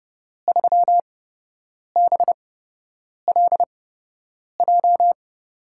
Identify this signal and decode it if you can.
Morse code, heard as a keyed tone, about 30 wpm, spaced as for 8 wpm, 700 Hz. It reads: 3BLJ
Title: USA Call Signs - 5 Characters - 8x Character Spacing - 30wpm